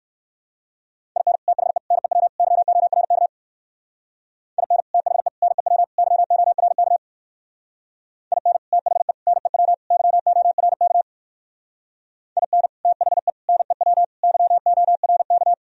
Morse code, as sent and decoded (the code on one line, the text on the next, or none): .. -. / - .... / -. .-- / -.-- --- .-. -.- / .. -. / - .... / -. .-- / -.-- --- .-. -.- / .. -. / - .... / -. .-- / -.-- --- .-. -.- / .. -. / - .... / -. .-- / -.-- --- .-. -.-
IN THE NEW YORK IN THE NEW YORK IN THE NEW YORK IN THE NEW YORK